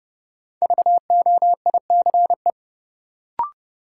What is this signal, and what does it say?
Morse code, heard as a keyed tone, about 30 wpm, 700 Hz. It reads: VOICE
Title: Sentences from Top 500 Words - New Word Drill - 30wpm